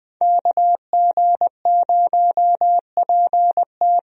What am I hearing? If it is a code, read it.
Morse code, heard as a keyed tone, about 20 wpm, 700 Hz. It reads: KG0PT
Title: USA Call Signs - 5 Characters - Repeated 3x - 20wpm